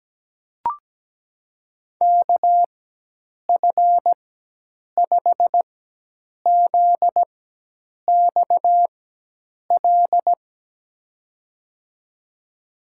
Morse code, heard as a keyed tone, about 17 wpm, spaced as for 8 wpm, 700 Hz. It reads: KF5ZXL
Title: USA Call Signs - 6 Characters - 4x Character Spacing - 17wpm